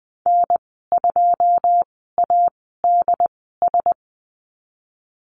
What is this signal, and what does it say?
Morse code, heard as a keyed tone, about 20 wpm, spaced as for 15 wpm, 700 Hz. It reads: N2ADS
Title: 088 Lesson 15g - Number 5 - Call Signs 20wpm